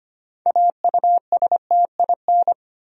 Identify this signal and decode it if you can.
Morse code, heard as a keyed tone, about 25 wpm, 700 Hz. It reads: AUSTIN
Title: USA Common Names - 25wpm